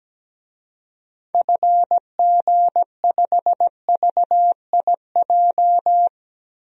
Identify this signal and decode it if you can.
Morse code, heard as a keyed tone, about 17 wpm, 700 Hz. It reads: FG5VIJ
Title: DXCC-100 Call Signs - 6 Characters - Repeated 2x - 17wpm